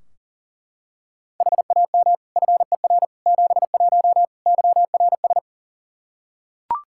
Morse code, heard as a keyed tone, about 40 wpm, 700 Hz. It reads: HAM FER 71 YRS
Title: K1USN SST QSO Practice 40wpm